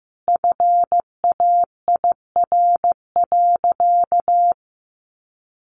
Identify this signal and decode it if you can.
Morse code, heard as a keyed tone, about 15 wpm, 700 Hz. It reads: FAIR.
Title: Sentences from Top 1300 Words 15wpm